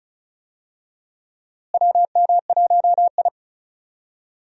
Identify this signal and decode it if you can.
Morse code, heard as a keyed tone, about 36 wpm, 700 Hz. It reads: WM1I
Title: USA Call Signs - 4 Characters - Repeated 3x - 35wpm